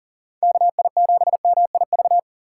Morse code, heard as a keyed tone, about 48 wpm, 700 Hz. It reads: KI7MIV
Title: Speed Racing - USA 2x3 Call Signs 40wpm